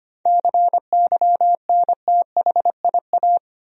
Morse code, heard as a keyed tone, about 25 wpm, 700 Hz. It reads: CYNTHIA
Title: USA Common Names - 25wpm